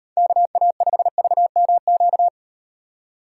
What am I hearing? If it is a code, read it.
Morse code, heard as a keyed tone, about 34 wpm, 700 Hz. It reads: KA5VMQ